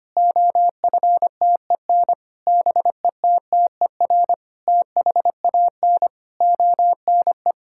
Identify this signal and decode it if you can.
Morse code, heard as a keyed tone, about 25 wpm, 700 Hz. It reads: OFTEN BETTER THAN ONE
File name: practice / 25wpm / Difficult Sentences with Hints - 25wpm